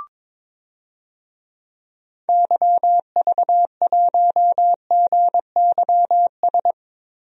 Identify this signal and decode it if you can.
Morse code, heard as a keyed tone, about 22 wpm, 700 Hz. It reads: YV1GYS
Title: DXCC-100 Call Signs - 6 Characters - 22wpm